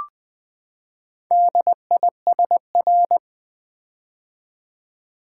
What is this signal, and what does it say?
Morse code, heard as a keyed tone, about 20 wpm, 700 Hz. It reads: DISR
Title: Build A Word - 4 Letters At A Time - 10-Letter Words - 20wpm